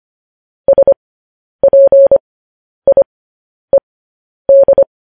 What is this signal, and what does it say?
Morse code, heard as a keyed tone, about 25 wpm, spaced as for 10 wpm, 550 Hz. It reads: SPIED
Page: news-headlines-in-morse-code-at-10-wpm